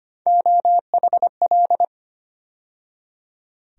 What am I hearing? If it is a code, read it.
Morse code, heard as a keyed tone, about 25 wpm, 700 Hz. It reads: OHL
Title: USA Call Sign Postfixes - 3 Letters 25wpm